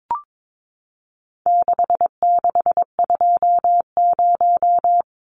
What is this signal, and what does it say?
Morse code, heard as a keyed tone, about 22 wpm, 700 Hz. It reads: 6620
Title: Numbers - 4-Digit 22wpm